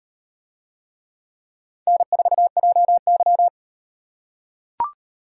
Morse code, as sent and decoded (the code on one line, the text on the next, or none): -. ....- .--- -.--
N4JY